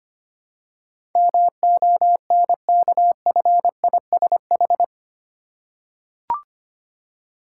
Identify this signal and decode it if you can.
Morse code, heard as a keyed tone, about 25 wpm, 700 Hz. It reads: MONKFISH
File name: practice / 25wpm / Difficult Words with Hints - 25wpm